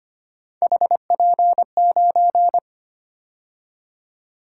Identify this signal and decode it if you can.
Morse code, heard as a keyed tone, about 25 wpm, 700 Hz. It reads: HP9
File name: Three Letter-Number Combinations - 25wpm